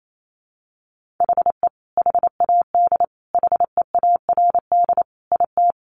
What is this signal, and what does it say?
Morse code, heard as a keyed tone, about 28 wpm, 700 Hz. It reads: HE HAD HEARD IT
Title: Sentences from Top 1400 Words 28wpm